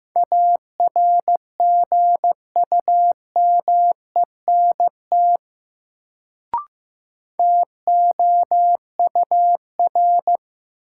Morse code, heard as a keyed tone, about 15 wpm, 700 Hz. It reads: ARGUMENT TOUR
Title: Sentences from Top 1400 Words - New Word Drill Alt - 15wpm